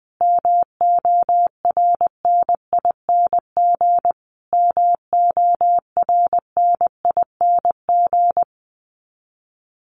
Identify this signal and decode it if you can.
Morse code, heard as a keyed tone, about 20 wpm, 700 Hz. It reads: MORNING MORNING
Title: Top 500 Words Repeated 20wpm